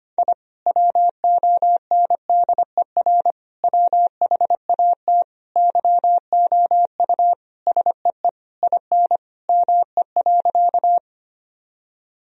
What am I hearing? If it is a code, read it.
Morse code, heard as a keyed tone, about 25 wpm, 700 Hz. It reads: I WONDER WHAT YOU SEE IN ME.
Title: Sentences from Top 1000 Words - Review Part 3 - 25wpm